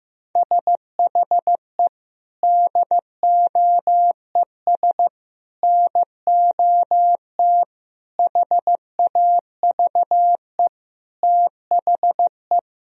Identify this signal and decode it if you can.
Morse code, heard as a keyed tone, about 15 wpm, 700 Hz. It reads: SHE DOES NOT HAVE THE
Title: Sentences from Top 1000 Words - Review Part 3 - 15wpm